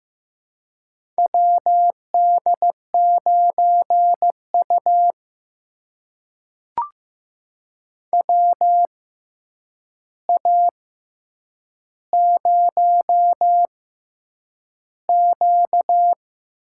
Morse code, heard as a keyed tone, about 15 wpm, spaced as for 8 wpm, 700 Hz. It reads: WD9U WA0Q
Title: USA Call Signs - 4 Characters - 6x Character Spacing - 15wpm